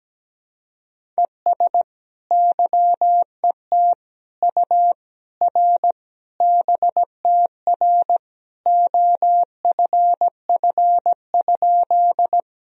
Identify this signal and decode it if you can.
Morse code, heard as a keyed tone, about 17 wpm, 700 Hz. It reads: ES YET U R BTR OFF?
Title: Sentences from Top 500 Words and Abbreviations 17wpm